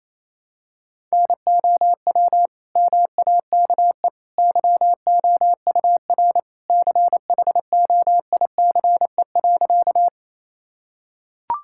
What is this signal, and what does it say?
Morse code, heard as a keyed tone, about 28 wpm, 700 Hz. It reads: NOW MAKE YOUR CHOICE.
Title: Sentences from Top 1000 Words - Review Part 6 - 28wpm